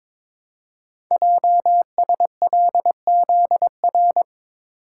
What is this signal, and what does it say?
Morse code, heard as a keyed tone, about 22 wpm, 700 Hz. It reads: JSLZR
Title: Code Groups - 5 Letters - 22wpm